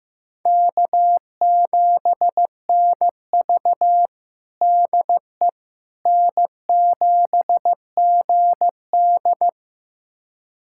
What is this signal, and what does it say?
Morse code, heard as a keyed tone, about 15 wpm, 700 Hz. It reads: K7NV DE N7GD
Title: USA Callsign DE Callsign - 4 Characters - 15wpm